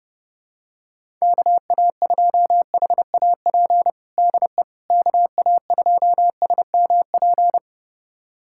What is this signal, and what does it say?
Morse code, heard as a keyed tone, about 30 wpm, 700 Hz. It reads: KA2HAP DE KA2SMP
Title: USA Callsign DE Callsign - 6 Characters - 30wpm